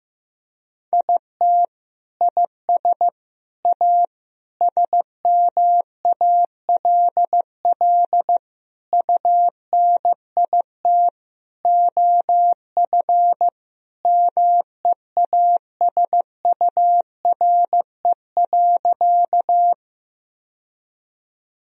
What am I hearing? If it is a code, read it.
Morse code, heard as a keyed tone, about 16 wpm, 700 Hz. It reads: IT IS A SMALL UNIT OF MEASURE.
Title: Sentences from Top 1000 Words - Review Part 2 - 15wpm